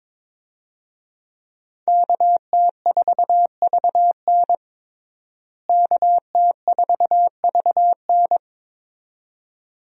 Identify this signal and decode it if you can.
Morse code, heard as a keyed tone, about 22 wpm, 700 Hz. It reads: KT4VN KT4VN
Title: USA Call Signs - 5 Characters - Repeated 2x - 22wpm